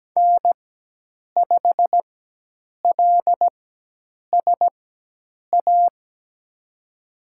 Morse code, heard as a keyed tone, about 17 wpm, spaced as for 8 wpm, 700 Hz. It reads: N5LSA
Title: USA Call Signs - 5 Characters - 4x Character Spacing - 17wpm